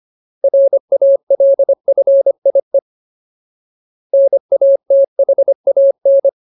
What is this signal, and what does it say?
Morse code, heard as a keyed tone, about 25 wpm, 550 Hz. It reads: RALFIE NATHAN
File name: Ultimate Word List / 25WPM Ultimate